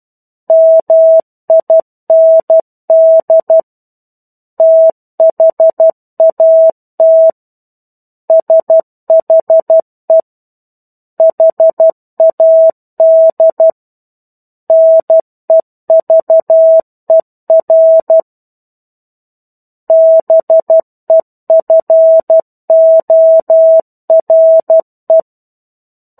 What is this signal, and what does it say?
Morse code, heard as a keyed tone, about 12 wpm, 650 Hz. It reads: MIND THAT SHE HAD NEVER BEFORE
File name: 01-Chapter-01 12 WPM